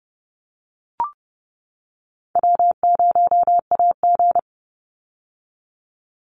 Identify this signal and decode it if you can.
Morse code, heard as a keyed tone, about 30 wpm, 700 Hz. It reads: W0AG
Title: Easy Call Signs - 30wpm